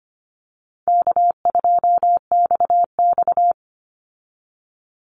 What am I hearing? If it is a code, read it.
Morse code, heard as a keyed tone, about 25 wpm, 700 Hz. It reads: K2XX
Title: Call Signs Repeated 2x - 25wpm